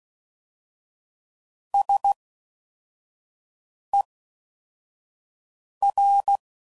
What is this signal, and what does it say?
Morse code, heard as a keyed tone, about 16 wpm, spaced as for 4 wpm, 800 Hz. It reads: SER